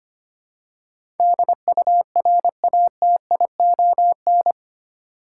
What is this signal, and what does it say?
Morse code, heard as a keyed tone, about 25 wpm, 700 Hz. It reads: DURATION
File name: Eight Letter Words - 25wpm